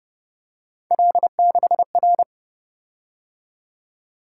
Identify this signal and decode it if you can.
Morse code, heard as a keyed tone, about 30 wpm, 700 Hz. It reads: L6R